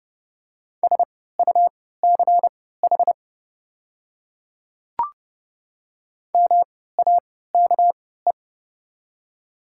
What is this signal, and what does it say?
Morse code, heard as a keyed tone, about 30 wpm, spaced as for 17 wpm, 700 Hz. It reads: SUCH MAKE
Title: 3x Character Spacing - Top 500 Words - 30wpm